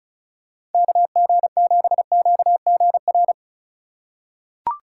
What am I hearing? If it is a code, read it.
Morse code, heard as a keyed tone, about 34 wpm, 700 Hz. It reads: KG7QGR
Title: USA Call Signs - 6 Characters - 4x Character Spacing - 35wpm